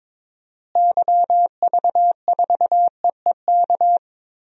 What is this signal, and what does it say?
Morse code, heard as a keyed tone, about 22 wpm, 700 Hz. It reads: YV4EEK
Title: DXCC-100 Call Signs - 6 Characters - Repeated 2x - 22wpm